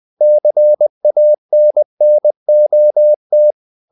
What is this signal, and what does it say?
Morse code, heard as a keyed tone, about 20 wpm, 600 Hz. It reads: CANNOT
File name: the old man and the sea part 2 20wpm